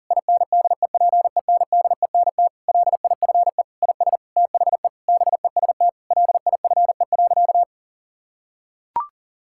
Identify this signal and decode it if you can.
Morse code, heard as a keyed tone, about 40 wpm, 700 Hz. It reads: INDEPENDENT LIFE IS THE BEST LIFE.
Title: Sentences from Top 1000 Words - Review Part 6 - 40wpm